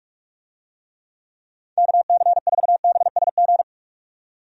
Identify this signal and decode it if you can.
Morse code, heard as a keyed tone, about 46 wpm, 700 Hz. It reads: KC4BSG
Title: USA Call Signs - 6 Characters - 45wpm